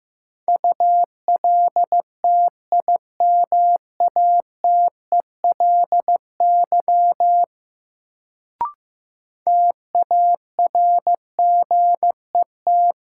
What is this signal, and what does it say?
Morse code, heard as a keyed tone, about 15 wpm, 700 Hz. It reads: ULTIMATELY TARGET
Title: Sentences from Top 1200 Words - New Word Drill - 15wpm